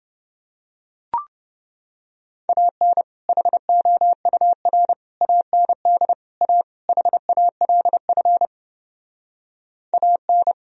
Morse code, heard as a keyed tone, about 29 wpm, 700 Hz. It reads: AN HOUR AND A HALF AN